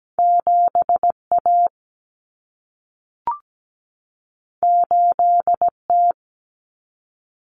Morse code, heard as a keyed tone, about 17 wpm, 700 Hz. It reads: 7A 8T
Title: Common Cut Numbers - 2-Digit 17wpm